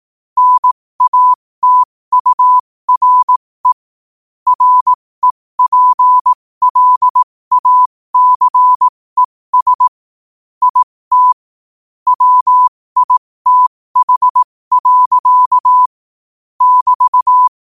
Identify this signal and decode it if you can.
Morse code, heard as a keyed tone, about 18 wpm, spaced as for 15 wpm, 1000 Hz. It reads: NATURE REPLACES IT WITH. <BT>
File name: qotdfiles / 15wpm